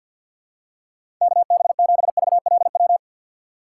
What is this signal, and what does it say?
Morse code, heard as a keyed tone, about 51 wpm, 700 Hz. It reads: KB7VLW
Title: USA Call Signs - 6x Character Spacing - 50wpm